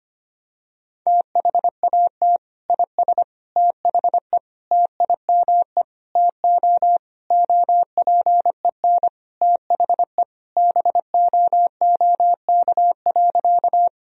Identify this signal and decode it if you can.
Morse code, heard as a keyed tone, about 25 wpm, 700 Hz. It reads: THAT IS THE TIME TO OPEN THE BOOK.
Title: Sentences from Top 1000 Words - Review Part 1 - 25wpm